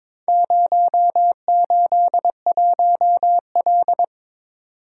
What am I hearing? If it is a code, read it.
Morse code, heard as a keyed tone, about 22 wpm, 700 Hz. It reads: 081L
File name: practice / 22wpm / Four Letter-Number Combinations - 22wpm